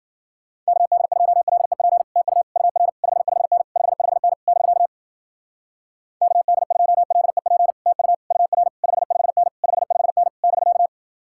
Morse code, heard as a keyed tone, about 50 wpm, 700 Hz. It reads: KD2LEP TU UR 55N 55N <BK> KD2LEP TU UR 55N 55N <BK>
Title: Speed-Racing Common QSO Phrases Encore - 40wpm